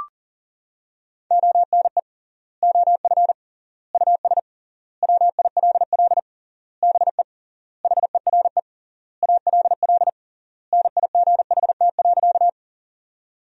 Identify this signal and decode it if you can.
Morse code, heard as a keyed tone, about 37 wpm, 700 Hz. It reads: ONE OF US WILL BE HERE ALL NIGHT.
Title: 3x Word-Spacing - Sentences from Top 300 Words - 40wpm